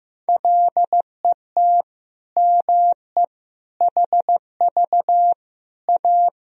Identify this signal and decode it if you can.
Morse code, heard as a keyed tone, about 16 wpm, 700 Hz. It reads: LET ME HV A
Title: Sentences from Top 300 Words and Abbreviations 15wpm